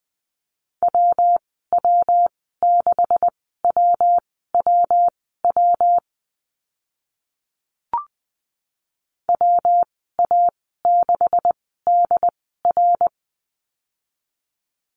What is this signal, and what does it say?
Morse code, heard as a keyed tone, about 20 wpm, spaced as for 14 wpm, 700 Hz. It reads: WW6WWW WA6DR
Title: 140 Lesson 23g - Letter f - Call Signs 20wpm